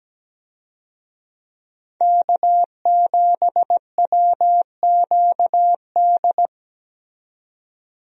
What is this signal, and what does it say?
Morse code, heard as a keyed tone, about 17 wpm, 700 Hz. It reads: K7WQD
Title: USA Call Signs - 5 Characters - 17wpm